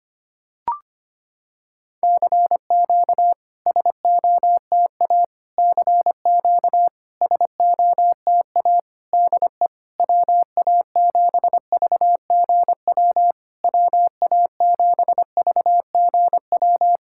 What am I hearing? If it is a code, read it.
Morse code, heard as a keyed tone, about 25 wpm, 700 Hz. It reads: CQ SOTA CQ SOTA DE WA7VGW WA7VGW
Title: SOTA QSO Practice 25wpm